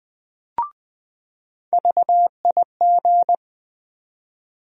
Morse code, heard as a keyed tone, about 20 wpm, 700 Hz. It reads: VIG
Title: USA Call Sign Postfixes - 3 Letters 20wpm